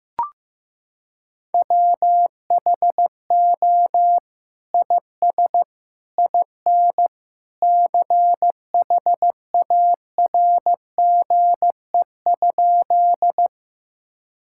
Morse code, heard as a keyed tone, about 16 wpm, 700 Hz. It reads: WHO IS IN CHARGE?
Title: Sentences from Top 1000 Words - Review Part 6 - 15wpm